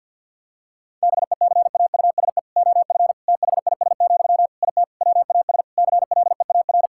Morse code, heard as a keyed tone, about 50 wpm, 700 Hz. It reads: BECAUSE OF THIS, IT WAS CLEAR
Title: Sentences from Top 1000 Words - Review Part 6 - 50wpm